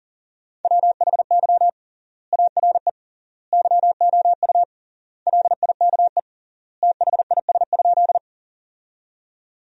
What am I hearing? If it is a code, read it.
Morse code, heard as a keyed tone, about 37 wpm, 700 Hz. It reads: WHY ARE YOU LIKE THIS?
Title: 3x Word-Spacing - Sentences from Top 200 Words - 40wpm